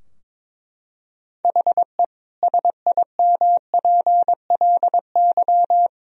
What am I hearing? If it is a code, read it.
Morse code, heard as a keyed tone, about 22 wpm, 700 Hz. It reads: HE SIMPLY
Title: Sentences from Top 1000 Words - Review Part 4 - 22wpm